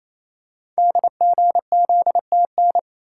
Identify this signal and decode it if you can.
Morse code, heard as a keyed tone, about 28 wpm, 700 Hz. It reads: DGZTN
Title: Code Groups - 5 Letter:Numbers - 28wpm